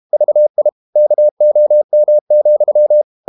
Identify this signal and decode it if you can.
Morse code, heard as a keyed tone, about 32 wpm, 600 Hz. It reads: VI KOM,
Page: Ferie 32 wpm